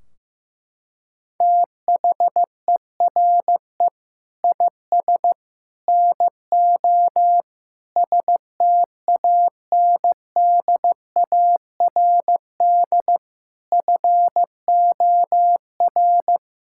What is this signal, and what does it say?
Morse code, heard as a keyed tone, about 15 wpm, 700 Hz. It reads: THERE IS NO STANDARD FOR
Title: Sentences from Top 1600 Words 15wpm